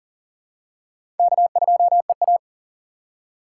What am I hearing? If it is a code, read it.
Morse code, heard as a keyed tone, about 40 wpm, 700 Hz. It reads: K2EA
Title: USA Call Signs - 4x Character Spacing - 40wpm